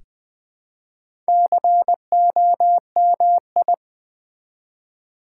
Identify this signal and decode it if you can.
Morse code, heard as a keyed tone, about 20 wpm, 700 Hz. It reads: COMI
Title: Build A Word - 1 Letter At A Time - 5-Letter Words - 20wpm